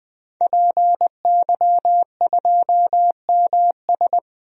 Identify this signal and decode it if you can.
Morse code, heard as a keyed tone, about 20 wpm, 700 Hz. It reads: PY2MS
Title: DXCC-100 Call Signs - 5 Characters - 20wpm